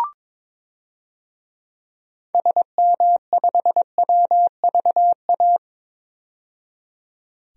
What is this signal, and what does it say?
Morse code, heard as a keyed tone, about 22 wpm, 700 Hz. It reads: SM5WVA
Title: DXCC-100 Call Signs - 6 Characters - 22wpm